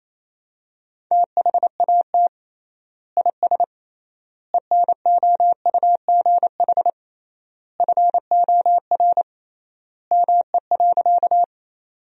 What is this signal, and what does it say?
Morse code, heard as a keyed tone, about 28 wpm, 700 Hz. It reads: THAT IS ENOUGH FOR ME.
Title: 3x Word-Spacing - Sentences from Top 400 Words - 28wpm